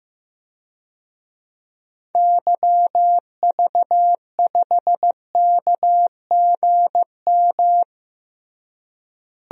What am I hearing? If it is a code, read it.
Morse code, heard as a keyed tone, about 15 wpm, 700 Hz. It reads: YV5KGM
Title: DXCC-100 Call Signs - 6 Characters - Repeated 3x - 15wpm